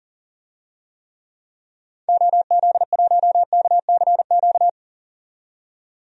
Morse code, heard as a keyed tone, about 37 wpm, 700 Hz. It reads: OZ1KCQ
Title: DXCC-100 Call Signs - 6 Characters - 40wpm